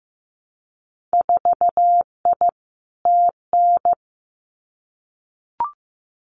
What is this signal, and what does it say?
Morse code, heard as a keyed tone, about 15 wpm, 700 Hz. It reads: 4I TN
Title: Winter Field Day Exchanges 15wpm